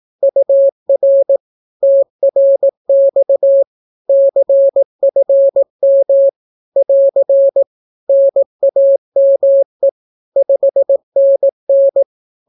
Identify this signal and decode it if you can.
Morse code, heard as a keyed tone, about 18 wpm, 550 Hz. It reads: UR TRX CFM <AR> NAME 5NN
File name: naukapodstaw / skroty / skroty